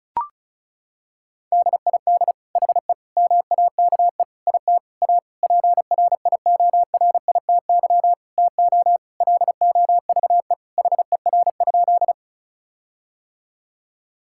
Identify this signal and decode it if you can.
Morse code, heard as a keyed tone, about 36 wpm, 700 Hz. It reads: DID HE MAKE IT A PRIORITY TO LOVE HER?
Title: Sentences from Top 1600 Words 35wpm